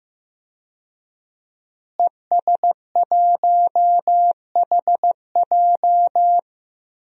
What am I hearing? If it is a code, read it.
Morse code, heard as a keyed tone, about 15 wpm, 700 Hz. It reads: ES1HJ
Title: DXCC-100 Call Signs - 5 Characters - Repeated 3x - 15wpm